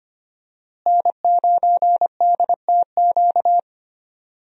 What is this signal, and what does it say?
Morse code, heard as a keyed tone, about 25 wpm, 700 Hz. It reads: N9DTQ
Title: USA Call Signs - 5 Characters - Repeated 3x - 25wpm